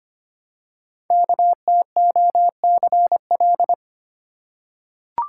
Morse code, heard as a keyed tone, about 25 wpm, 700 Hz. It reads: KTOCL
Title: Code Groups - 5 Letters - 25wpm